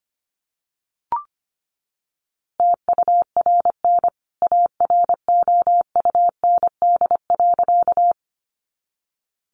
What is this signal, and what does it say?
Morse code, heard as a keyed tone, about 25 wpm, 700 Hz. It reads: TURN AROUND.
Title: A Sound Of Thunder 25wpm